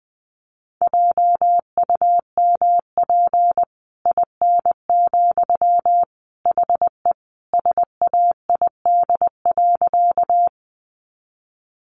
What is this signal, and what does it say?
Morse code, heard as a keyed tone, about 20 wpm, 700 Hz. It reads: JUMP IN, HE SAID.
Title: Sentences from Top 1300 Words 20wpm